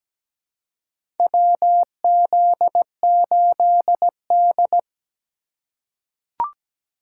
Morse code, heard as a keyed tone, about 17 wpm, 700 Hz. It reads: WZ8D
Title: USA Call Signs - 4x Character Spacing - 17wpm